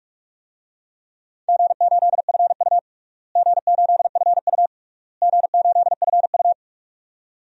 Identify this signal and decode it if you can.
Morse code, heard as a keyed tone, about 46 wpm, 700 Hz. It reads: G8FU G8FU G8FU
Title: DXCC-100 Call Signs - 4 Characters - Repeated 3x - 45wpm